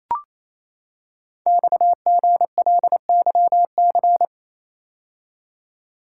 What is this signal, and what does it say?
Morse code, heard as a keyed tone, about 29 wpm, 700 Hz. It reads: XGLYC
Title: Code Groups - 5 Letters - 28wpm